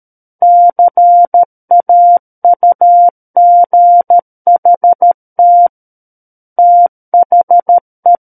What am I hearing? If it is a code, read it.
Morse code, heard as a keyed tone, about 13 wpm, 700 Hz. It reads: CAUGHT THE
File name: Book One / 04-Chapter 4 - 13 WPM